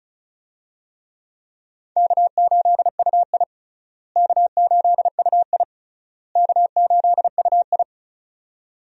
Morse code, heard as a keyed tone, about 36 wpm, 700 Hz. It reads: K8UI K8UI K8UI
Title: USA Call Signs - 4 Characters - Repeated 3x - 35wpm